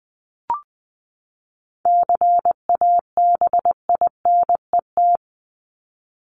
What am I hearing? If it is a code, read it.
Morse code, heard as a keyed tone, about 20 wpm, 700 Hz. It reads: CABINET
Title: Top 1751-2000 Words 20wpm